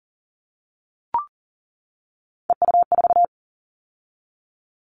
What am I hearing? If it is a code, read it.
Morse code, heard as a keyed tone, about 40 wpm, 700 Hz. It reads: EU4
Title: Callsign Prefixes for Top 50 DXCC - 40wpm